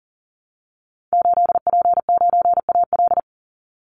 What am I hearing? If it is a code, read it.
Morse code, heard as a keyed tone, about 40 wpm, 700 Hz. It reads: 8P9AL
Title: Call Signs Repeated 2x - 40wpm